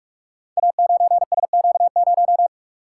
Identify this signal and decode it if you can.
Morse code, heard as a keyed tone, about 44 wpm, 700 Hz. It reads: A9SQ0
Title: Code Groups - 5 Letter:Numbers - 45wpm